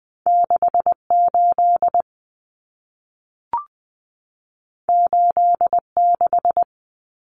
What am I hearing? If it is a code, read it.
Morse code, heard as a keyed tone, about 20 wpm, 700 Hz. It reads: 68 86
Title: Common Cut Numbers - 2-Digit 20wpm